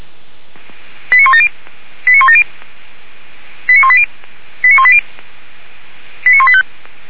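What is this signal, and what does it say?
Five-tone selective call (ZVEI), heard as quick runs of five tones, tones of about 80 ms. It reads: 89170 89170 89180 89180 89185
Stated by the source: ZVEI five-tone call 89136 received by SDR, from feuerwehr-orlamuende fire brigade server.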